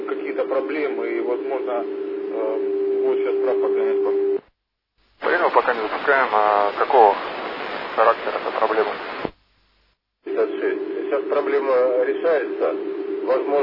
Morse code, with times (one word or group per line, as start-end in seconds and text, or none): none